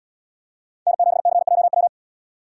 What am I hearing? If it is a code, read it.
Morse code, heard as a keyed tone, about 76 wpm, 700 Hz. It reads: N7CJW